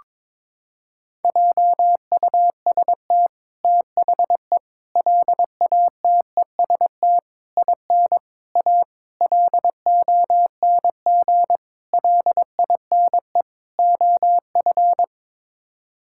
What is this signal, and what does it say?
Morse code, heard as a keyed tone, about 22 wpm, 700 Hz. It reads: JUST THE LATEST IN A LONG LINE OF
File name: practice / 22wpm / Sets of 8 Words - 22wpm